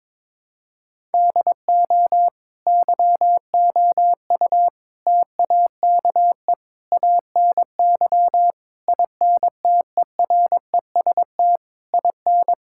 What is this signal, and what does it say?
Morse code, heard as a keyed tone, about 22 wpm, 700 Hz. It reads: DO YOU TAKE ANY INTEREST IN
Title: Sentences from Top 1000 Words - Review Part 6 - 22wpm